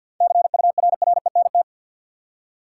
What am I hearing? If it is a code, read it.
Morse code, heard as a keyed tone, about 52 wpm, 700 Hz. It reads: CURRENT
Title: Seven Letter Words - 50wpm